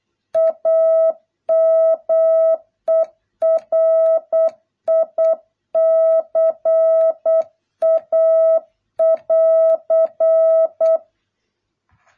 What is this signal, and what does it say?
Morse code, heard as a keyed tone, about 8 wpm, 650 Hz. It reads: AMERICA <AR>